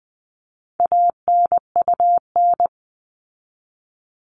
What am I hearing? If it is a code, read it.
Morse code, heard as a keyed tone, about 20 wpm, 700 Hz. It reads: ANUN
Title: All Cut Numbers - 4-Digit 20wpm